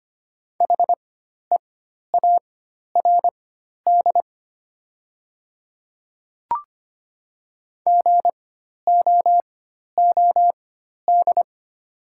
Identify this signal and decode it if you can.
Morse code, heard as a keyed tone, about 25 wpm, spaced as for 12 wpm, 700 Hz. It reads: HEARD GOOD
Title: 4x Character Spacing - Top 500 Words - 25wpm